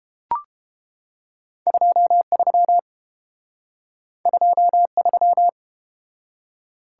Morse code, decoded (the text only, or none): 23 23